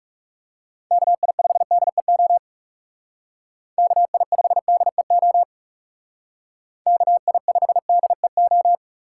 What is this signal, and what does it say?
Morse code, heard as a keyed tone, about 37 wpm, 700 Hz. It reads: KI5DEO KI5DEO KI5DEO
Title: Speed Racing - USA 2x3 Call Signs 30wpm